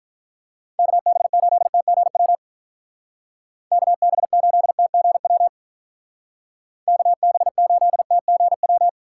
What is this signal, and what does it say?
Morse code, heard as a keyed tone, about 44 wpm, 700 Hz. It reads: KB8TGW KB8TGW KB8TGW